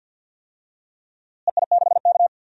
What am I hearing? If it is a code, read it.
Morse code, heard as a keyed tone, about 50 wpm, 700 Hz. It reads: EI6K
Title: DXCC-100 Call Signs - 4 Characters - Repeated 2x - 50wpm